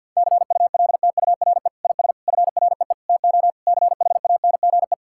Morse code, heard as a keyed tone, about 50 wpm, 700 Hz. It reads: CULTURE IS FREE TO CHANGE
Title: Sentences from Top 1000 Words - Review Part 6 - 50wpm